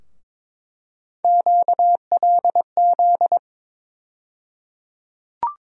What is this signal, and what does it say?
Morse code, heard as a keyed tone, about 22 wpm, 700 Hz. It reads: QLZ
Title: USA Call Sign Postfixes - 3 Letters 22wpm